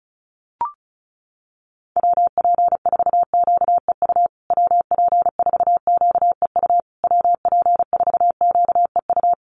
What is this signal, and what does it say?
Morse code, heard as a keyed tone, about 34 wpm, 700 Hz. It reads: WP4QEU WP4QEU WP4QEU
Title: USA Call Signs Repeated 3x 35wpm